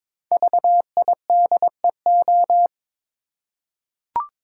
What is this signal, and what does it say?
Morse code, heard as a keyed tone, about 22 wpm, 700 Hz. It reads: VIDEO
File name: 5 Letter Words - First 3 Letters Given - 22wpm